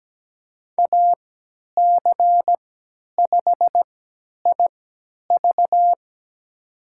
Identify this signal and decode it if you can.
Morse code, heard as a keyed tone, about 17 wpm, spaced as for 10 wpm, 700 Hz. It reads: AC5IV
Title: USA Call Signs - 5 Characters - 3x Character Spacing - 17wpm